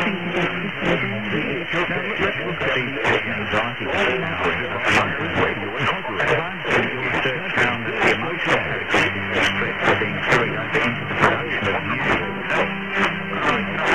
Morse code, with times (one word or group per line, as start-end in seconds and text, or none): none